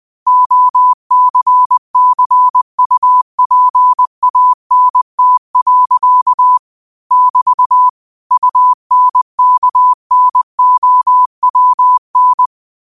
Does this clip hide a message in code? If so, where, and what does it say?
Yes: Morse code from 0.3 to 12.5 s, heard as a keyed tone, about 20 wpm, 1000 Hz. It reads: OCCUPANT. <BT> UNKNOWN